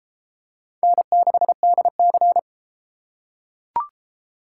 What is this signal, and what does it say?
Morse code, heard as a keyed tone, about 29 wpm, 700 Hz. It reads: N6DC